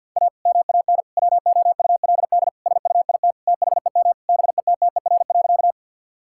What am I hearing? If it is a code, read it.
Morse code, heard as a keyed tone, about 50 wpm, 700 Hz. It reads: A MAN WOULD SUIT THEM BETTER.
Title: Sentences from Top 1600 Words 50wpm